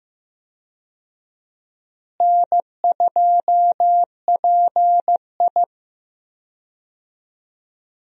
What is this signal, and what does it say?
Morse code, heard as a keyed tone, about 14 wpm, 700 Hz. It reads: N2PI
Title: USA Call Signs - 4 Characters - Repeated 3x - 15wpm